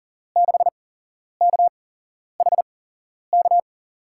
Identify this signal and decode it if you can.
Morse code, heard as a keyed tone, about 38 wpm, spaced as for 11 wpm, 700 Hz. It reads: 6KHK
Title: USA Call Signs - 6 Characters - 8x Character Spacing - 40wpm